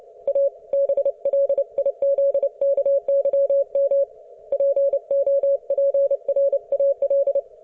Morse code, heard as a keyed tone, about 29 wpm, 560 Hz. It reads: A BLIZKYM POPRAL